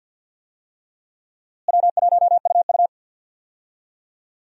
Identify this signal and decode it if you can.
Morse code, heard as a keyed tone, about 52 wpm, 700 Hz. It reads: W1UU
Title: USA Call Signs - 4 Characters - Repeated 3x - 50wpm